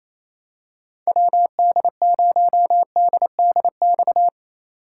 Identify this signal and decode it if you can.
Morse code, heard as a keyed tone, about 28 wpm, 700 Hz. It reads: WD0DDX
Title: USA Call Signs - 6 Characters - 2x Character Spacing - 28wpm